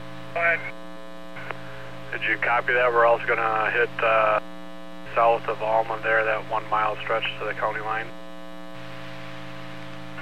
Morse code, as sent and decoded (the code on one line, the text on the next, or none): none